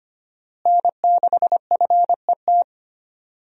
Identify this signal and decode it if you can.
Morse code, heard as a keyed tone, about 25 wpm, 700 Hz. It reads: N6FET
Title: USA Call Signs - 5 Characters - Repeated 2x - 25wpm